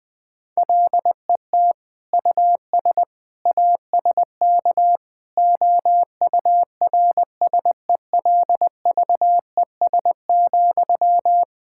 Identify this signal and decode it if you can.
Morse code, heard as a keyed tone, about 21 wpm, 700 Hz. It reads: LET US ASK OURSELVES,
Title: Sentences from Top 1600 Words 20wpm